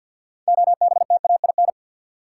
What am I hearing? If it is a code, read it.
Morse code, heard as a keyed tone, about 53 wpm, 700 Hz. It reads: OBTAIN
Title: Sentences from Top 1300 Words - New Word Drill - 50wpm